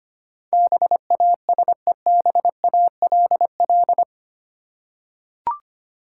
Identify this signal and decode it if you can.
Morse code, heard as a keyed tone, about 25 wpm, 700 Hz. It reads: BASEBALL
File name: practice / 25wpm / Easy Words with Hints - 25wpm